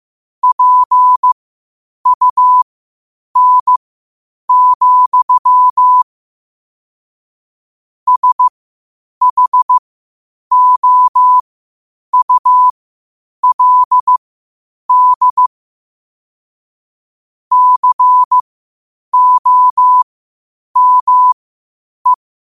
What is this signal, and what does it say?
Morse code, heard as a keyed tone, about 15 wpm, spaced as for 8 wpm, 1000 Hz. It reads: PUN, SHOULD COME
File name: 8wpm